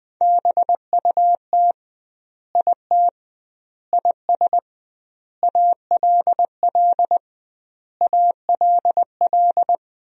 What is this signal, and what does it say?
Morse code, heard as a keyed tone, about 20 wpm, 700 Hz. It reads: BUT IT IS ALL ALL
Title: 2x Word-Spacing - Sentences from Top 300 Words - 20wpm